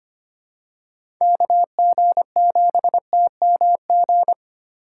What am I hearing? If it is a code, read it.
Morse code, heard as a keyed tone, about 25 wpm, 700 Hz. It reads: KG7TMG
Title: USA Call Signs - 6 Characters - 2x Character Spacing - 25wpm